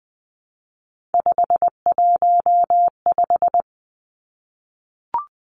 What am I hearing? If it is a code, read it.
Morse code, heard as a keyed tone, about 20 wpm, 700 Hz. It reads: 515
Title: Three Challenging Letter-Number Combinations 20wpm